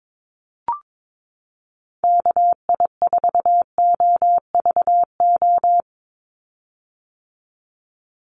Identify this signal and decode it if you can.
Morse code, heard as a keyed tone, about 22 wpm, 700 Hz. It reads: KI4OVO
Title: USA Call Signs 22wpm